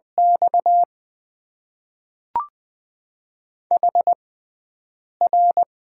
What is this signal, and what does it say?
Morse code, heard as a keyed tone, about 20 wpm, spaced as for 7 wpm, 700 Hz. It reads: X HR
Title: Top 90 QSO Elements - 6x Character Spacing - 20wpm